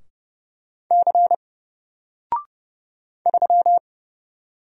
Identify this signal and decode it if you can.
Morse code, heard as a keyed tone, about 30 wpm, 700 Hz. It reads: C 3
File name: Single-Letter-Number-30wpm